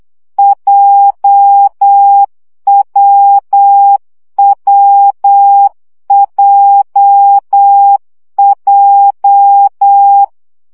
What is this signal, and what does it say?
Morse code, heard as a keyed tone, about 8 wpm, 800 Hz. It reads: JWWJJ
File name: CW gyakorlas w p j 7 baud